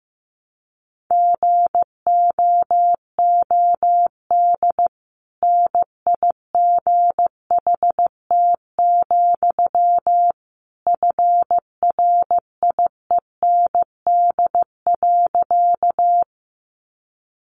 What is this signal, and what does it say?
Morse code, heard as a keyed tone, about 15 wpm, 700 Hz. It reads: GOOD NIGHT, FRIEND.
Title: Sentences from Top 300 Words 15wpm